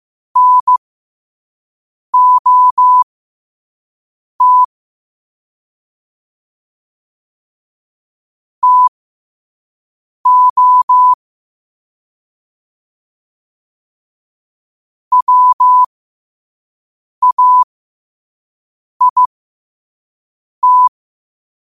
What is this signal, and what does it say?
Morse code, heard as a keyed tone, about 15 wpm, spaced as for 5 wpm, 1000 Hz. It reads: NOT TO WAIT
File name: qotdfiles / 5wpm